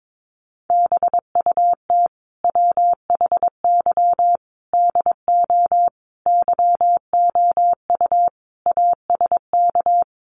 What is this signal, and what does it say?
Morse code, heard as a keyed tone, about 22 wpm, 700 Hz. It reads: BUT WHY DO YOU ASK
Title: Sentences from Top 200 Words 22wpm